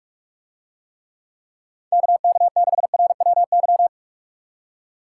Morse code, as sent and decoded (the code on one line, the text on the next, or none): -.- -.- -.... .-. .-- -.--
KK6RWY